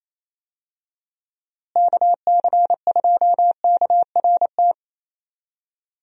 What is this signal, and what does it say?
Morse code, heard as a keyed tone, about 27 wpm, 700 Hz. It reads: KC2KRT
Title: USA Call Signs - 6 Characters - Repeated 2x - 28wpm